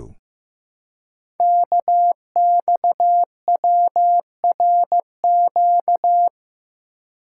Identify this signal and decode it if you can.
Morse code, heard as a keyed tone, about 15 wpm, 700 Hz. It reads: KXWRQ
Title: Code Groups - 5 Letters - 15wpm